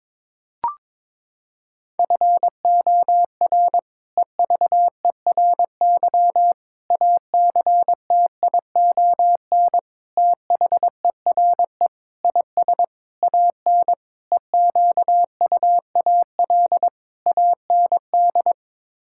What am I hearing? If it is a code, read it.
Morse code, heard as a keyed tone, about 22 wpm, 700 Hz. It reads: FOR EVERY ACTION THERE IS AN EQUAL AND
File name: practice / 22wpm / Sets of 8 Words - 22wpm